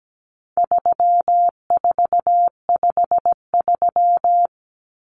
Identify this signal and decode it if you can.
Morse code, heard as a keyed tone, about 17 wpm, 700 Hz. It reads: 3453
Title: Numbers - 4-Digit 17wpm